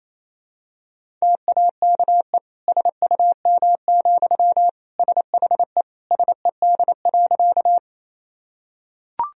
Morse code, heard as a keyed tone, about 28 wpm, 700 Hz. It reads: TAKE SUM, SHE SED.
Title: Sentences from Top 200 Words and Abbreviations 28wpm